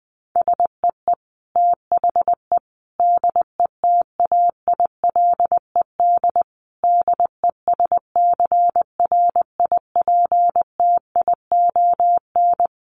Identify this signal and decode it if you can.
Morse code, heard as a keyed tone, about 20 wpm, 700 Hz. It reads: SEE THE DETAILED DESCRIPTION
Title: Sentences from Top 1500 Words 20wpm